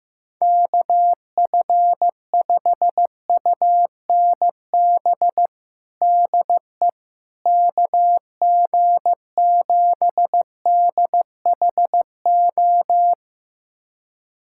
Morse code, heard as a keyed tone, about 15 wpm, 700 Hz. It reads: KF5UNB DE KG7DHO
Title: USA Callsign DE Callsign - 6 Characters - 15wpm